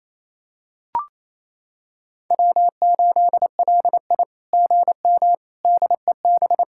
Morse code, heard as a keyed tone, about 28 wpm, 700 Hz. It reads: W8LI GM DEB
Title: SOTA QSO Practice 28wpm